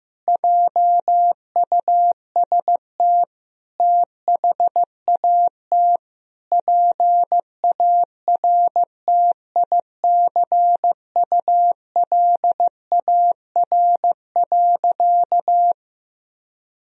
Morse code, heard as a keyed tone, about 15 wpm, 700 Hz. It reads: JUST THAT PARTICULAR.
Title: Sentences from Top 1000 Words - Review Part 4 - 15wpm